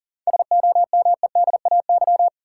Morse code, heard as a keyed tone, about 40 wpm, 700 Hz. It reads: SOMEDAY
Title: Sentences from Top 1600 Words - New Word Drill Alt - 40wpm